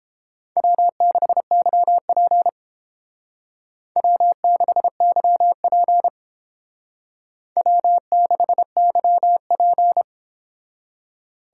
Morse code, heard as a keyed tone, about 29 wpm, 700 Hz. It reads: W6YP W6YP W6YP